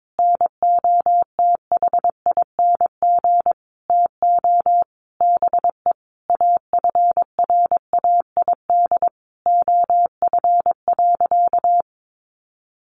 Morse code, heard as a keyed tone, about 22 wpm, 700 Hz. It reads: NOTHING TO BE AFRAID OF.
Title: Sentences from Top 1500 Words 22wpm